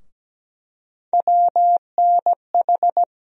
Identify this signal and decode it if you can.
Morse code, heard as a keyed tone, about 17 wpm, 700 Hz. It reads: WNH
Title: USA Call Sign Postfixes - 3 Letters 17wpm